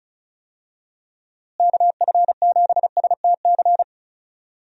Morse code, heard as a keyed tone, about 36 wpm, 700 Hz. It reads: KF7STC
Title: USA Call Signs - 6 Characters - Repeated 3x - 35wpm